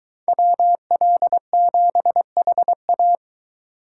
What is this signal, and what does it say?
Morse code, heard as a keyed tone, about 22 wpm, 700 Hz. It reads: WL7HA